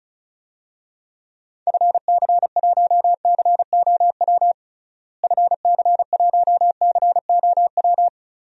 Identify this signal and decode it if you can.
Morse code, heard as a keyed tone, about 36 wpm, 700 Hz. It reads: FC1COW FC1COW
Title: DXCC-100 Call Signs - 6 Characters - Repeated 3x - 35wpm